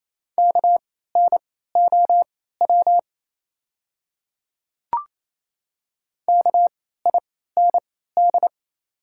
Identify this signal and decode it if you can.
Morse code, heard as a keyed tone, about 28 wpm, spaced as for 16 wpm, 700 Hz. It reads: KNOW KIND
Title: 3x Character Spacing - Top 200 Words - 28wpm